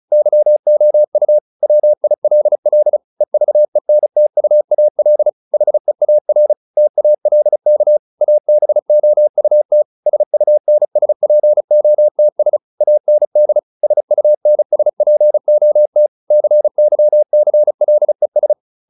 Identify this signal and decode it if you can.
Morse code, heard as a keyed tone, about 35 wpm, 600 Hz. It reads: YOU WILL EVENTUAL HEAR TALK ABOUT SUNSPOTS AND SUNSPOT CYCLES